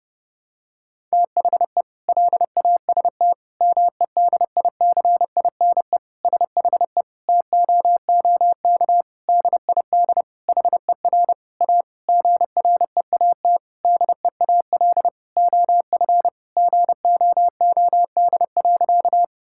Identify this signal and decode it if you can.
Morse code, heard as a keyed tone, about 32 wpm, 700 Hz. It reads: THE LAST MEDICINE SHE TOOK DID HER A GREAT DEAL OF GOOD.
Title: Sentences from Top 1000 Words - Review Part 6 - 30wpm